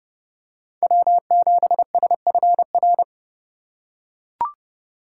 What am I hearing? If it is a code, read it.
Morse code, heard as a keyed tone, about 29 wpm, 700 Hz. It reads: W7SFR